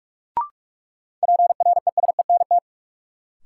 Morse code, heard as a keyed tone, about 46 wpm, 700 Hz. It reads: PRESENT
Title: Sentences from Top 800 Words - New Word Drill - 45wpm